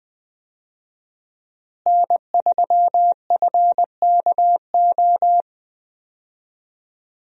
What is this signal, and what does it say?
Morse code, heard as a keyed tone, about 20 wpm, 700 Hz. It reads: N3FKO
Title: USA Call Signs - 5 Characters - Repeated 2x - 20wpm